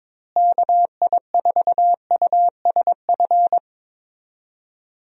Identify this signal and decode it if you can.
Morse code, heard as a keyed tone, about 22 wpm, 700 Hz. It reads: KI4USF
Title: USA Call Signs - 6 Characters - Repeated 2x - 22wpm